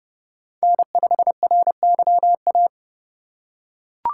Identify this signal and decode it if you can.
Morse code, heard as a keyed tone, about 30 wpm, 700 Hz. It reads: N5RYA
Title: USA Call Signs - 5 Characters - 4x Character Spacing - 30wpm